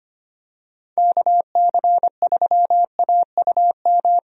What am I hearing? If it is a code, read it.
Morse code, heard as a keyed tone, about 25 wpm, 700 Hz. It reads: KC3AUM
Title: USA Call Signs - 6 Characters - Repeated 2x - 25wpm